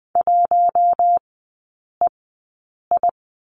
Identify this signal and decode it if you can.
Morse code, heard as a keyed tone, about 20 wpm, spaced as for 8 wpm, 700 Hz. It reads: 1EI
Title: 074 Lesson 13e - Letter L - Three Characters 20wpm